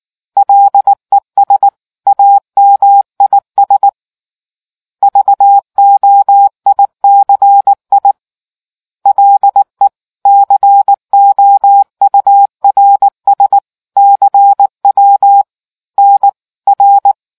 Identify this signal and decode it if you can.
Morse code, heard as a keyed tone, about 19 wpm, 800 Hz. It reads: LES AMIS VOICI LE COURS CW NR